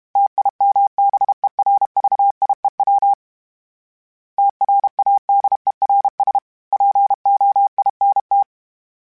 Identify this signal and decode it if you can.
Morse code, heard as a keyed tone, about 32 wpm, 800 Hz. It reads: TIMBERVIEW TRADERS POINT
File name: fromSMs / FromSMs mp3s / Cities@30